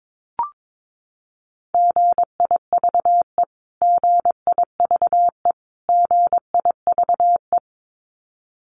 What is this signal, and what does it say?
Morse code, heard as a keyed tone, about 22 wpm, 700 Hz. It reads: GIVE GIVE GIVE
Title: Top 200 Words Repeated 22wpm